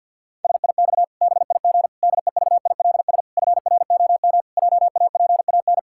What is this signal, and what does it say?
Morse code, heard as a keyed tone, about 50 wpm, 700 Hz. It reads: SIX BIG DEVILS FROM JAPAN
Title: Pangram Challenge v2 - 50wpm